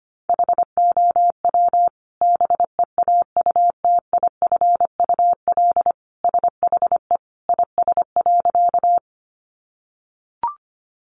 Morse code, heard as a keyed tone, about 25 wpm, 700 Hz. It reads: HOW BEAUTIFUL SHE IS.
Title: Sentences from Top 900 Words 25wpm